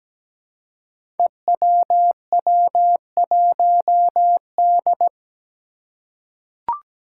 Morse code, heard as a keyed tone, about 17 wpm, 700 Hz. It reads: EWW1D
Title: Five Challenging Letter-Number Combinations 17wpm